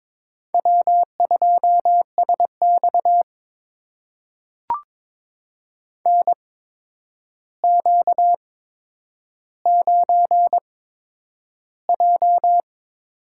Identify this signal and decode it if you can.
Morse code, heard as a keyed tone, about 22 wpm, spaced as for 9 wpm, 700 Hz. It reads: W2SX NQ9J
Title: USA Call Signs - 8x Character Spacing - 22wpm